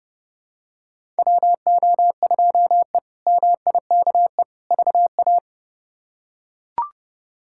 Morse code, heard as a keyed tone, about 30 wpm, 700 Hz. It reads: WO2E MIKE VA
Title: K1USN SST QSO Practice - 30wpm